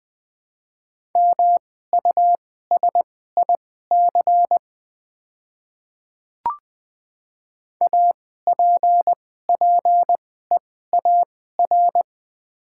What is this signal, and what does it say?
Morse code, heard as a keyed tone, about 20 wpm, spaced as for 14 wpm, 700 Hz. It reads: MUSIC APPEAR
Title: 2x Character Spacing - Top 500 Words - 20wpm